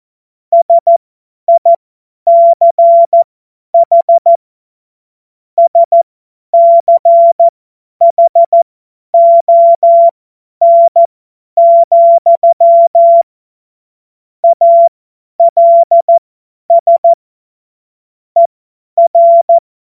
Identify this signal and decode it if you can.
Morse code, heard as a keyed tone, about 14 wpm, spaced as for 10 wpm, 680 Hz. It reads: SICH SCHON, ALS ER
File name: Morsekurs / Witze / Witz 04. Januar 2022, 50 F70